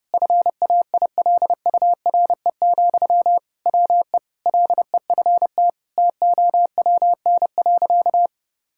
Morse code, heard as a keyed tone, about 28 wpm, 700 Hz. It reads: FAILURE, WE LEFT TOWN.
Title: Sentences from Top 1000 Words - Review Part 6 - 30wpm